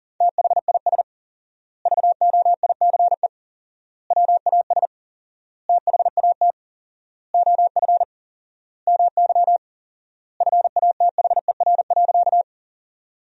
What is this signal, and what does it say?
Morse code, heard as a keyed tone, about 40 wpm, 700 Hz. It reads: THIS VOICE WAS THAT OF MY FATHER.
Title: 4x Word-Spacing - Sentences from Top 500 Words - 40wpm